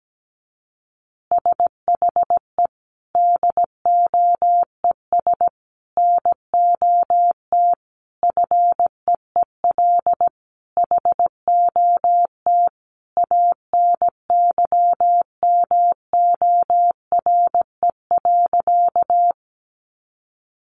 Morse code, heard as a keyed tone, about 17 wpm, 700 Hz. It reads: SHE DOES NOT FEEL HOT ANYMORE.
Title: Additional Easy Phrases 17wpm